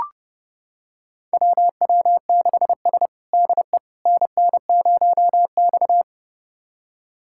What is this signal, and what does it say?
Morse code, heard as a keyed tone, about 30 wpm, 700 Hz. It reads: WW6S DE NN0X
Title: USA Callsign DE Callsign - 4 Characters - 30wpm